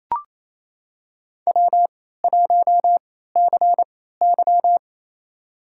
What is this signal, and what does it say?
Morse code, heard as a keyed tone, about 28 wpm, spaced as for 16 wpm, 700 Hz. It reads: W1CY
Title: USA Call Signs - 4 Characters - 3x Character Spacing - 28wpm